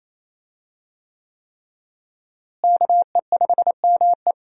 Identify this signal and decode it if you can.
Morse code, heard as a keyed tone, about 28 wpm, 700 Hz. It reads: KE5ME
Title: USA Call Signs - 5 Characters - Repeated 3x - 28wpm